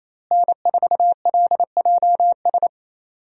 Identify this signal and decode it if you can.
Morse code, heard as a keyed tone, about 28 wpm, 700 Hz. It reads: N4LJS
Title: USA Call Signs - 5 Characters - Repeated 3x - 28wpm